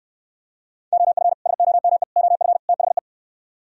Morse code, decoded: OF SOME OF THE

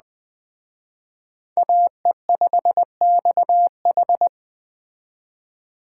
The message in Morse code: AE5XH

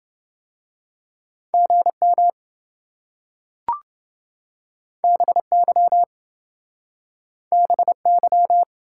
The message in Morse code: GM BY BY

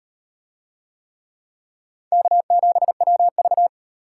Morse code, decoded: K7WV

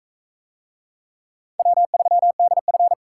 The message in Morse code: W3DF

